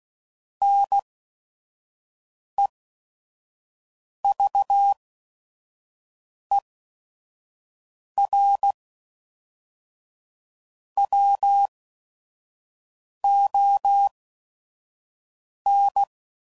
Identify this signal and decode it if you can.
Morse code, heard as a keyed tone, about 16 wpm, spaced as for 5 wpm, 800 Hz. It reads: NEVER WON